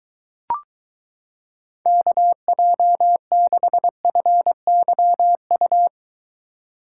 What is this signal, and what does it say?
Morse code, heard as a keyed tone, about 23 wpm, 700 Hz. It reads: KJ6FYU